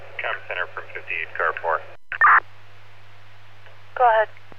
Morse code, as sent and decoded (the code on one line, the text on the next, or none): none